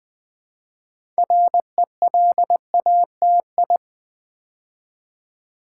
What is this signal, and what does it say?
Morse code, heard as a keyed tone, about 20 wpm, 700 Hz. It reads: RELATI
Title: Build A Word - 2 Letters At A Time - 8-Letter Words - 20wpm